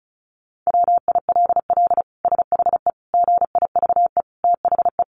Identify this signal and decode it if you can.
Morse code, heard as a keyed tone, about 34 wpm, 700 Hz. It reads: WILL SHE GIVE THE
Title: Sentences from Top 1400 Words 35wpm